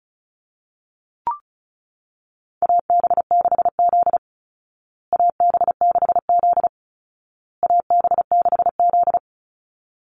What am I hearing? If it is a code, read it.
Morse code, heard as a keyed tone, about 34 wpm, 700 Hz. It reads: AB6Z AB6Z AB6Z
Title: Call Signs Repeated 2x - 35wpm